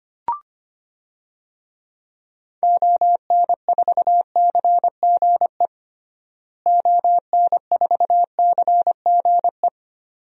Morse code, heard as a keyed tone, about 25 wpm, 700 Hz. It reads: ON4CGE ON4CGE
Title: DXCC-100 Call Signs - 6 Characters - Repeated 2x - 25wpm